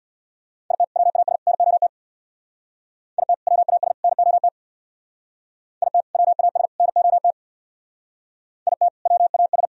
Morse code, heard as a keyed tone, about 58 wpm, 700 Hz. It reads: IT WAS NOT IT WAS NOT IT WAS NOT IT WAS
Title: Speed-Racing - Sets of 3 Words 50wpm